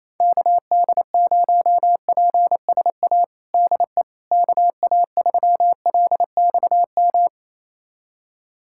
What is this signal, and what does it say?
Morse code, heard as a keyed tone, about 28 wpm, 700 Hz. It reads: KD0PSA DE KA3LXM
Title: USA Callsign DE Callsign - 6 Characters - 28wpm